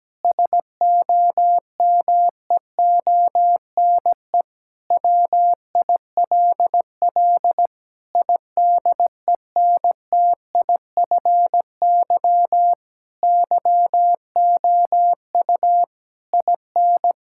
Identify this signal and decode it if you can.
Morse code, heard as a keyed tone, about 17 wpm, 700 Hz. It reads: SOMEONE WILL IDENTIFY YOU IN
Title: Sentences from Top 1000 Words - Review Part 6 - 17wpm